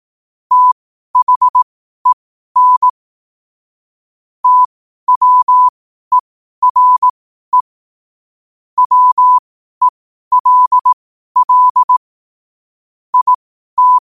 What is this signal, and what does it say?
Morse code, heard as a keyed tone, about 18 wpm, spaced as for 12 wpm, 1000 Hz. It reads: THEN TWERE WELL IT
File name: qotdfiles / 12wpm